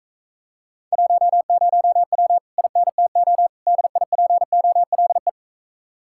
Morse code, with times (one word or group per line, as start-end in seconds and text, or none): 0.9-2.4 10W
2.6-3.5 INTO
3.7-5.3 DIPOLE